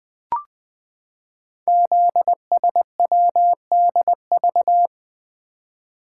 Morse code, heard as a keyed tone, about 20 wpm, 700 Hz. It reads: ZSWDV
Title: Code Groups - 5 Letters - 20wpm